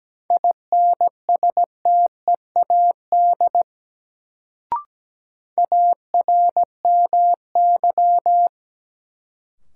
Morse code, heard as a keyed tone, about 17 wpm, 700 Hz. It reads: INSTEAD ARMY
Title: Sentences from Top 800 Words - New Word Drill - 17wpm